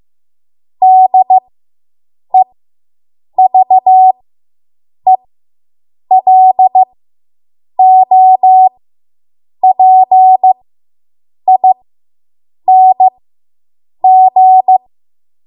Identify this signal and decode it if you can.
Morse code, heard as a keyed tone, about 15 wpm, spaced as for 7 wpm, 750 Hz. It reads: DEVELOPING